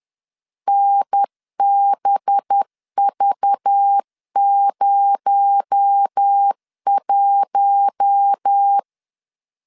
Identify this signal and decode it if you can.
Morse code, heard as a keyed tone, about 10 wpm, 800 Hz. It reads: NBV01
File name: LEZIONI / 6 PKX0 / 6 PKX0 10wpm 2707